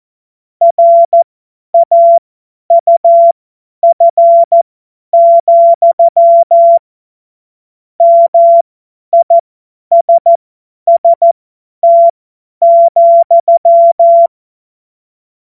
Morse code, heard as a keyed tone, about 14 wpm, spaced as for 10 wpm, 680 Hz. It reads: RAUF, MISST,
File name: Morsekurs / Witze / Witz 28. Dezember 2021, 50 F70